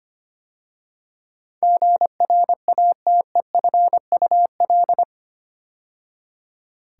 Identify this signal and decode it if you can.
Morse code, heard as a keyed tone, about 25 wpm, 700 Hz. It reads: GRATEFUL
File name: 8 Letter Words - First 3 Letters Given - 25wpm